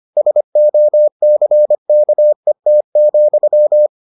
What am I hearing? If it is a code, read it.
Morse code, heard as a keyed tone, about 25 wpm, 600 Hz. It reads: SOCKET,